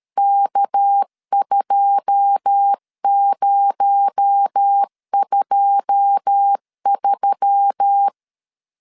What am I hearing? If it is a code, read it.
Morse code, heard as a keyed tone, about 13 wpm, 800 Hz. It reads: K2023
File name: LEZIONI / 9 2 12wpm 2301